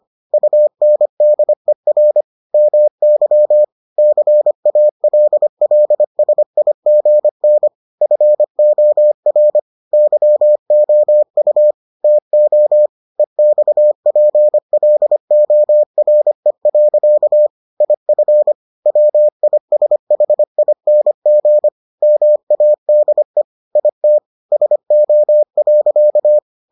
Morse code, heard as a keyed tone, about 25 wpm, 600 Hz. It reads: UNDER MY CALLSIGN FOR YOU TO EXPLORE. IF WISHING MADE IT SO.